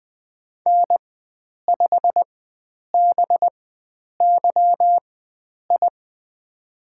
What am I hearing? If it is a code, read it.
Morse code, heard as a keyed tone, about 20 wpm, spaced as for 9 wpm, 700 Hz. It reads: N5BYI
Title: USA Call Signs - 4x Character Spacing - 20wpm